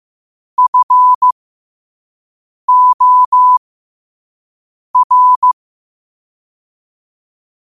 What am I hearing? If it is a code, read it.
Morse code, heard as a keyed tone, about 15 wpm, spaced as for 5 wpm, 1000 Hz. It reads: FOR
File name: qotdfiles / 5wpm